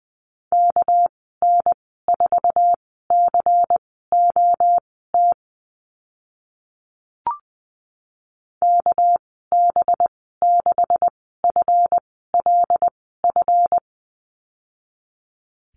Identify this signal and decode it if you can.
Morse code, heard as a keyed tone, about 20 wpm, spaced as for 14 wpm, 700 Hz. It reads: KN4COT KB6FLF
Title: 219 Lesson 35g - Number 0 - Call Signs 20wpm